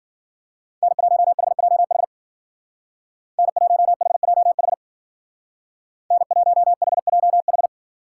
Morse code, heard as a keyed tone, about 50 wpm, 700 Hz. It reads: N1HJH N1HJH N1HJH
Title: Speed Racing - USA 1x3 Call Signs 40wpm